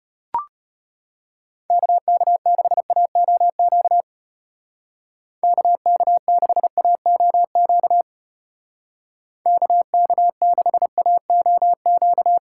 Morse code, heard as a keyed tone, about 31 wpm, 700 Hz. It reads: KK6AOQ KK6AOQ KK6AOQ
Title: Speed Racing - USA 2x3 Call Signs 25wpm